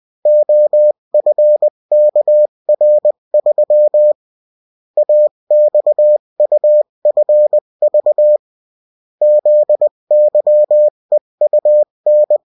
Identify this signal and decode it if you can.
Morse code, heard as a keyed tone, about 20 wpm, 600 Hz. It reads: OFKR3 AXUFV ZYEUN
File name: morse mp3 / revision9-vitesse 18 mots